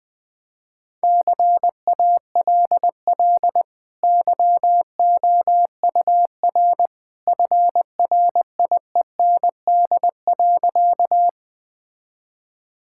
Morse code, heard as a keyed tone, about 20 wpm, 700 Hz. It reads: CALL YOUR FRIEND.
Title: Easy Sentences with Hints - 20wpm